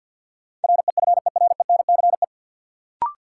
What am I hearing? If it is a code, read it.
Morse code, heard as a keyed tone, about 53 wpm, 700 Hz. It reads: REFERENCE
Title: Sentences from Top 1200 Words - New Word Drill Alt - 50wpm